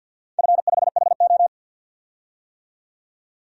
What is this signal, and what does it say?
Morse code, heard as a keyed tone, about 53 wpm, 700 Hz. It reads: F5HO
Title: DXCC-100 Call Signs - 4 Characters - 50wpm